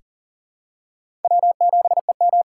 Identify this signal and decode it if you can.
Morse code, heard as a keyed tone, about 40 wpm, 700 Hz. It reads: W7EM
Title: USA Call Signs - 4 Characters - 4x Character Spacing - 40wpm